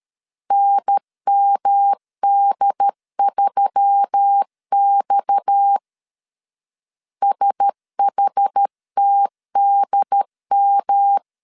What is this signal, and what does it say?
Morse code, heard as a keyed tone, about 13 wpm, 800 Hz. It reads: NMD3X SHTDM
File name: CW / DET / 1 6 1500C S A 12wpm 3440